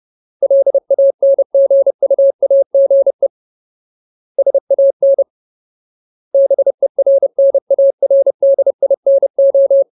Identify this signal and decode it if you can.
Morse code, heard as a keyed tone, about 30 wpm, 550 Hz. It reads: LANGUAGE SAN BERNARDINO
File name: Ultimate Word List / 30WPM Ultimate